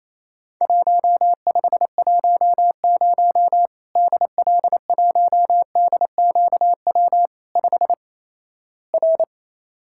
Morse code, heard as a keyed tone, about 27 wpm, 700 Hz. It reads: 1510 DL1DQW 5 R
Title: Worked All Europe DX Contest - QTC Exchange - 28wpm